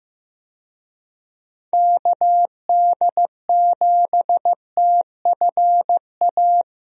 Morse code, heard as a keyed tone, about 15 wpm, 700 Hz. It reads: KD7TFA
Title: USA Call Signs - 6 Characters - Repeated 3x - 15wpm